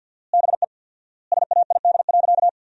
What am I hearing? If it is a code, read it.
Morse code, heard as a keyed tone, about 53 wpm, 700 Hz. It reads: BE SAID.
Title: 4x Word-Spacing - Sentences from Top 500 Words - 50wpm